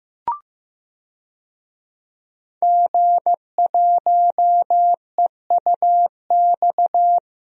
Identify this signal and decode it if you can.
Morse code, heard as a keyed tone, about 15 wpm, 700 Hz. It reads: G1EUX
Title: DXCC-100 Call Signs - 5 Characters - Repeated 2x - 15wpm